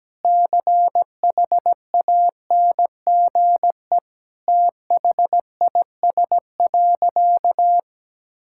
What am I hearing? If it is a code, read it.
Morse code, heard as a keyed tone, about 17 wpm, 700 Hz. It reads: CHANGE THIS.
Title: Sentences from Top 1000 Words - Review Part 6 - 17wpm